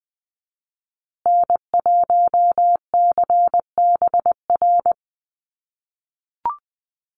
Morse code, heard as a keyed tone, about 20 wpm, 700 Hz. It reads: N1CBR
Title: USA Call Signs Repeated 3x 20wpm